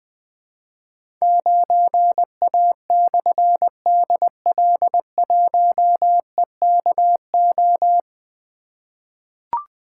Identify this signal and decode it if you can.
Morse code, heard as a keyed tone, about 20 wpm, 700 Hz. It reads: 9A/DL1EKO
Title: Difficult Call Signs 20wpm